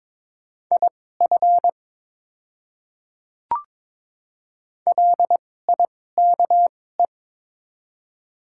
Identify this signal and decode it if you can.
Morse code, heard as a keyed tone, about 23 wpm, spaced as for 16 wpm, 700 Hz. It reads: IF LIKE
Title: 2x Character Spacing - Top 100 Words - 22wpm